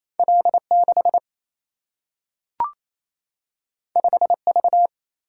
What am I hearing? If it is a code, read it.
Morse code, heard as a keyed tone, about 28 wpm, 700 Hz. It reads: L6 5V
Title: Single-Letter Single-Number 28wpm